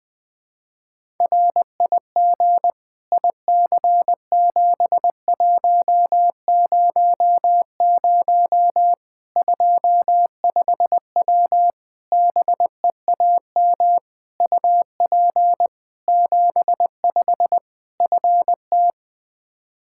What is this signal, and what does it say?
Morse code, heard as a keyed tone, about 21 wpm, 700 Hz. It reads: RIG IC7100 25W BEAM UP 75 FT
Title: Practice QSO - Exchange 2 - Info - 20wpm